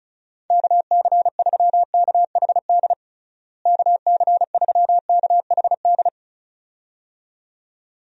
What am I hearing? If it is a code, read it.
Morse code, heard as a keyed tone, about 36 wpm, 700 Hz. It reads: KC3KHD KC3KHD
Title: USA Call Signs - 6 Characters - Repeated 2x - 35wpm